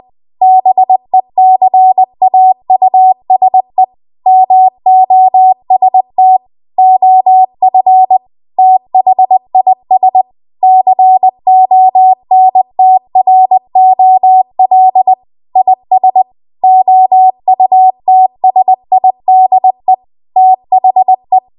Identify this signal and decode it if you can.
Morse code, heard as a keyed tone, about 20 wpm, 750 Hz. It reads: BECAUSE MOST OF THIS CONTROL IS OUTSIDE THE